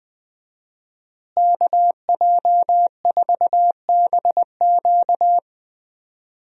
Morse code, heard as a keyed tone, about 20 wpm, 700 Hz. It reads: KJ4BQ
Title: USA Call Signs - 5 Characters - Repeated 2x - 20wpm